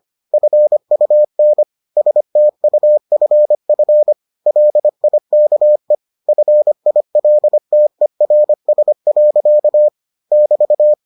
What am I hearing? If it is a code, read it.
Morse code, heard as a keyed tone, about 25 wpm, 600 Hz. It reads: FUN STUFF LIKE FILTERS. <BT>